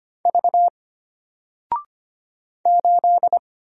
Morse code, heard as a keyed tone, about 25 wpm, 700 Hz. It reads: V 8
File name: Single-Letter-Number-25wpm